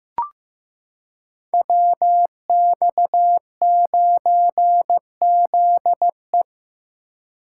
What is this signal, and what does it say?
Morse code, heard as a keyed tone, about 15 wpm, 700 Hz. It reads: WX9ZE
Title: Code Groups - 5 Letter:Numbers - 15wpm